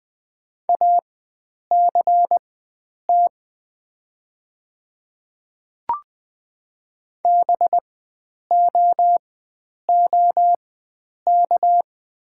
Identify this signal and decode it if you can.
Morse code, heard as a keyed tone, about 20 wpm, spaced as for 9 wpm, 700 Hz. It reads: ACT BOOK
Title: 4x Character Spacing - Top 300 Words - 20wpm